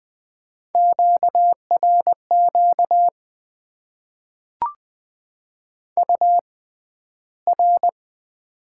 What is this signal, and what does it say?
Morse code, heard as a keyed tone, about 20 wpm, spaced as for 12 wpm, 700 Hz. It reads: QRQ UR
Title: Top 90 QSO Elements - 6x Character Spacing - 20wpm